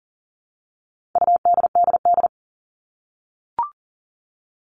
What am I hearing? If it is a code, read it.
Morse code, heard as a keyed tone, about 40 wpm, 700 Hz. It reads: UDDD
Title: All Cut Numbers - 4-Digit 40wpm